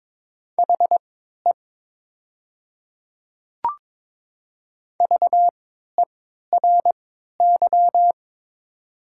Morse code, heard as a keyed tone, about 22 wpm, spaced as for 13 wpm, 700 Hz. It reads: HE VERY
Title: 3x Character Spacing - Top 200 Words - 22wpm